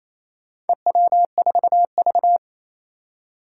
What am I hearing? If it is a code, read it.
Morse code, heard as a keyed tone, about 27 wpm, 700 Hz. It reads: EW4V
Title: Four Challenging Letter-Number Combinations 28wpm